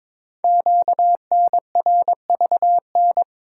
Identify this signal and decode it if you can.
Morse code, heard as a keyed tone, about 22 wpm, 700 Hz. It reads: QNRVN
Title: Code Groups - 5 Letters - 22wpm